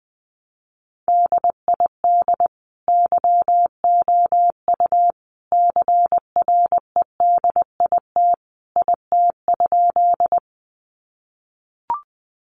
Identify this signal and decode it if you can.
Morse code, heard as a keyed tone, about 20 wpm, 700 Hz. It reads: DID YOU CREDIT IT?
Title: Sentences from Top 1000 Words 20wpm